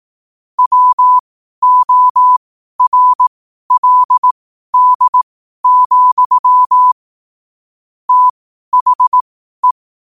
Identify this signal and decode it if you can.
Morse code, heard as a keyed tone, about 18 wpm, spaced as for 12 wpm, 1000 Hz. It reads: WORLD, THE
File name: qotdfiles / 12wpm